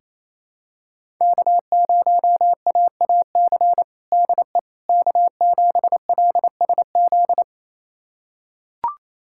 Morse code, heard as a keyed tone, about 27 wpm, 700 Hz. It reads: K0AAC DE K7LSZ
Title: USA Callsign DE Callsign - 5 Characters - 28wpm